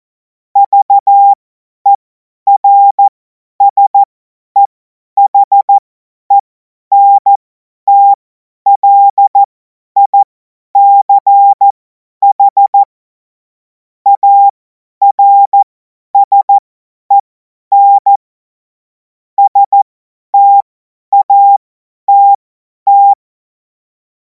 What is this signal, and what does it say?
Morse code, heard as a keyed tone, about 14 wpm, spaced as for 10 wpm, 800 Hz. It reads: VERSEHENTLICH ARSEN STATT
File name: Morsekurs / Witze / Witz 19. Mai 2026 50 F70